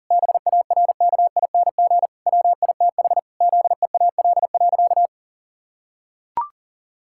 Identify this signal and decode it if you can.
Morse code, heard as a keyed tone, about 40 wpm, 700 Hz. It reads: BARKING WITH ZEAL.
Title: Pangram Challenge v2 - 40wpm